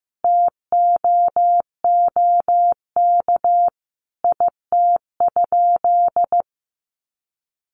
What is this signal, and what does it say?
Morse code, heard as a keyed tone, about 15 wpm, 700 Hz. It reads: TOOK IT?
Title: Sentences from Top 300 Words 15wpm